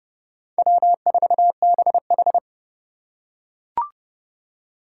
Morse code, heard as a keyed tone, about 27 wpm, 700 Hz. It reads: W4BH